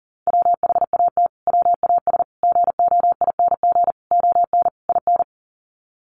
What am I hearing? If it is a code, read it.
Morse code, heard as a keyed tone, about 37 wpm, 700 Hz. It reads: WHAT WAS GOING ON IN
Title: Sets of 5 Words Encore 40wpm